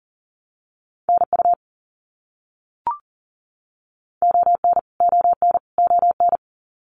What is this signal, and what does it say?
Morse code, heard as a keyed tone, about 40 wpm, 700 Hz. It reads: NU ON ON ON
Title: Canadian Provinces and Territories - Repeated 3x - 40wpm